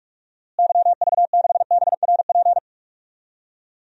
Morse code, heard as a keyed tone, about 42 wpm, 700 Hz. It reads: YV6BRP
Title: DXCC-100 Call Signs - 6 Characters - Repeated 2x - 45wpm